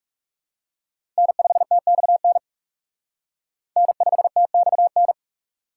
Morse code, heard as a keyed tone, about 36 wpm, 700 Hz. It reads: N5TXN N5TXN
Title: Speed Racing - USA Call Signs 30wpm